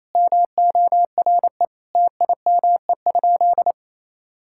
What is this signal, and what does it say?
Morse code, heard as a keyed tone, about 27 wpm, 700 Hz. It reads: MORE TIME?
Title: Sentences from Top 1000 Words - Review Part 5 - 28wpm